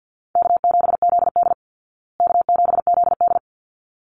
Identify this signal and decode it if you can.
Morse code, heard as a keyed tone, about 50 wpm, 700 Hz. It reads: K7ZD K7ZD
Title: Easy Call Signs Repeated 2x - 50wpm